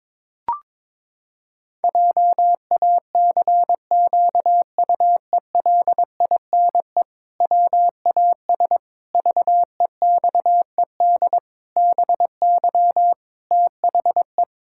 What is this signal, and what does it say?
Morse code, heard as a keyed tone, about 22 wpm, 700 Hz. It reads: JACQUELINE WAS VEXED BY THE
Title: Pangram Challenge v2 - 22wpm